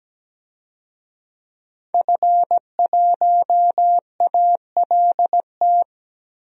Morse code, heard as a keyed tone, about 17 wpm, 700 Hz. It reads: F1ALT
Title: DXCC-100 Call Signs - 5 Characters - Repeated 3x - 17wpm